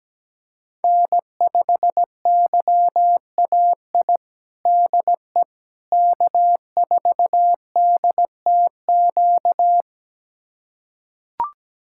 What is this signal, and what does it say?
Morse code, heard as a keyed tone, about 17 wpm, 700 Hz. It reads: N5YAI DE K4DTQ
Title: USA Callsign DE Callsign 17wpm